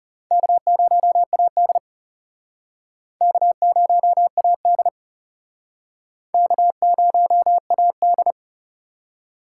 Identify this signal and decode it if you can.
Morse code, heard as a keyed tone, about 38 wpm, 700 Hz. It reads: K0AD K0AD K0AD